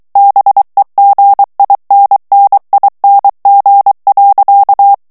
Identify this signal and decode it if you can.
Morse code, heard as a keyed tone, about 23 wpm, 800 Hz. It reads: BEGINNING.
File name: morse mp3 / ANGLAIS texte clair 2 vitesse 22 mots minute